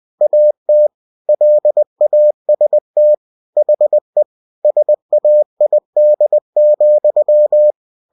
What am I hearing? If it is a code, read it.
Morse code, heard as a keyed tone, about 20 wpm, 600 Hz. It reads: AT LAST HE SAID,
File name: cw-books / winnie-the-pooh / winnie-20wpm-page-0027